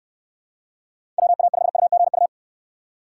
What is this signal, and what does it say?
Morse code, heard as a keyed tone, about 69 wpm, 700 Hz. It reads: WA5UGU